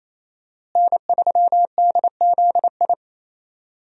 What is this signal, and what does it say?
Morse code, heard as a keyed tone, about 27 wpm, 700 Hz. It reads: N3DZI
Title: USA Call Signs - 5 Characters - Repeated 2x - 28wpm